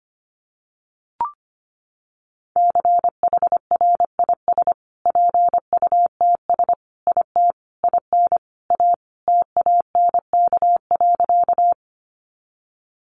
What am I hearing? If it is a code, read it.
Morse code, heard as a keyed tone, about 25 wpm, 700 Hz. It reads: CHRIS PUTS IT IN A TANK.
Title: Additional Easy Phrases 25wpm